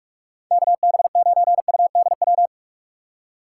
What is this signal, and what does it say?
Morse code, heard as a keyed tone, about 41 wpm, 700 Hz. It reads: KB9UDW